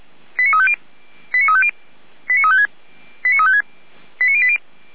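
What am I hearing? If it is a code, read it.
Five-tone selective call (ZVEI), heard as quick runs of five tones, tones of about 70 ms. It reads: 89350 89350 89356 89356 89080
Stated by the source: ZVEI five-tone selective call received on an SDR.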